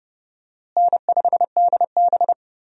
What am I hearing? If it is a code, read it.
Morse code, heard as a keyed tone, about 32 wpm, 700 Hz. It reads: N5DB
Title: USA Call Signs - 4 Characters - Repeated 3x - 30wpm